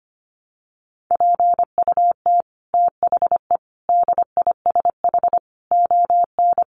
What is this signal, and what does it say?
Morse code, heard as a keyed tone, about 25 wpm, 700 Hz. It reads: PUT THE DISH ON
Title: Sentences from Top 1300 Words 25wpm